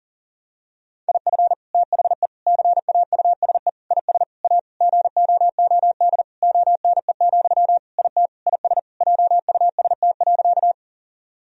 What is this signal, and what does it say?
Morse code, heard as a keyed tone, about 37 wpm, 700 Hz. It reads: IF THE CAUSE IS A GOOD ONE, IT IS JUST.
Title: Sentences from Top 1000 Words - Review Part 1 - 40wpm